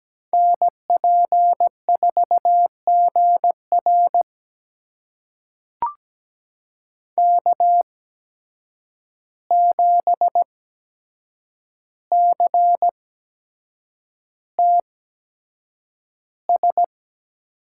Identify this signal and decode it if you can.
Morse code, heard as a keyed tone, about 17 wpm, spaced as for 7 wpm, 700 Hz. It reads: NP4GR K7CTS
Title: USA Call Signs - 5 Characters - 8x Character Spacing - 17wpm